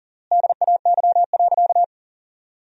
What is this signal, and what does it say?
Morse code, heard as a keyed tone, about 40 wpm, 700 Hz. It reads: DAY.
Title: 3x Word-Spacing - Sentences from Top 500 Words - 40wpm